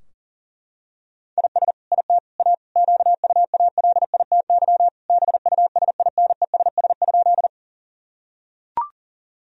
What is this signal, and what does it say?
Morse code, heard as a keyed tone, about 40 wpm, 700 Hz. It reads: IS IT A QUALITY BUSINESS?
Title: Sentences from Top 1000 Words - Review Part 4 - 40wpm